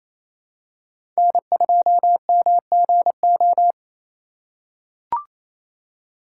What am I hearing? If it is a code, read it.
Morse code, heard as a keyed tone, about 28 wpm, 700 Hz. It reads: N2MGO
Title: USA Call Signs - 8x Character Spacing - 28wpm